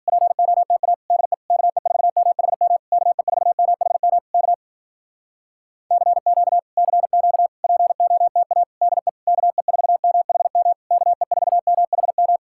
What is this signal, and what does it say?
Morse code, heard as a keyed tone, about 52 wpm, 700 Hz. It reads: POTA DE KE4MHM KE4MHM K CQ CQ POTA DE KE4MHM KE4MHM